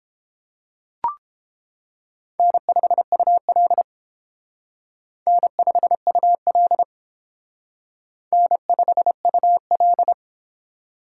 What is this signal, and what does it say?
Morse code, heard as a keyed tone, about 30 wpm, 700 Hz. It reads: N5UL N5UL N5UL